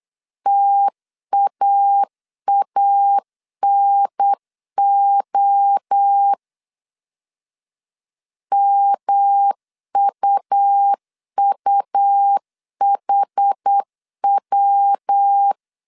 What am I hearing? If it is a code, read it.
Morse code, heard as a keyed tone, about 8 wpm, 800 Hz. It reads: TAANO MUUHW
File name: CW / DET / 1 3 1500C S B 8wpm 4141